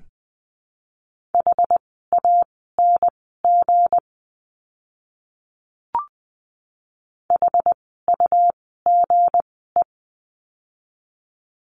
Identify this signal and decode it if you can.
Morse code, heard as a keyed tone, about 20 wpm, spaced as for 14 wpm, 700 Hz. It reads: HANG HUGE
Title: 160 Lesson 26f - Letter g - Words 20wpm